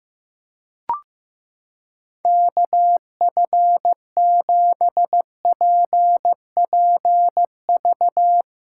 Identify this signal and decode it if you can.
Morse code, heard as a keyed tone, about 15 wpm, 700 Hz. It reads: KF7PPV
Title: USA Callsign DE Callsign 15wpm